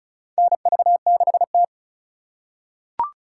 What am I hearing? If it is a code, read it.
Morse code, heard as a keyed tone, about 34 wpm, 700 Hz. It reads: NV6T
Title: USA Call Signs - 2x Character Spacing - 35wpm